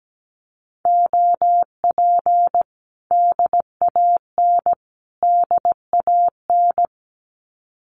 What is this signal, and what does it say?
Morse code, heard as a keyed tone, about 17 wpm, 700 Hz. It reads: OP DAN DAN
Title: Common QSO Phrases Encore 17wpm